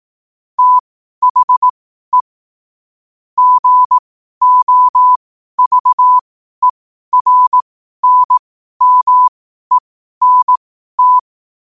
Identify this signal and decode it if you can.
Morse code, heard as a keyed tone, about 18 wpm, spaced as for 12 wpm, 1000 Hz. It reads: THE GOVERNMENT
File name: qotdfiles / 12wpm